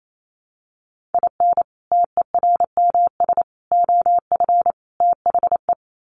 Morse code, heard as a keyed tone, about 28 wpm, 700 Hz. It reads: IN TERMS OF THE
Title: Sets of 4 Words 28wpm